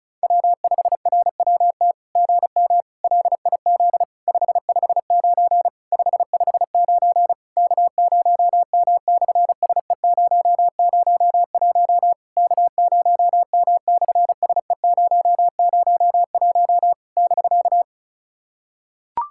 Morse code, heard as a keyed tone, about 36 wpm, 700 Hz. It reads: W5RWT GM LIZ 559 559 K0M/SE001 K0M/SE001 <BK>
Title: SOTA QSO Practice 35wpm